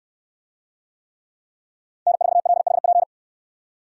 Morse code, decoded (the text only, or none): N4LHP